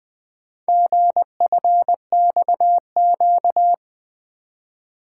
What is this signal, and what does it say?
Morse code, heard as a keyed tone, about 24 wpm, 700 Hz. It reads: GFXQ
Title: Speed-Racing - Four Letter-Number Combinations - 20wpm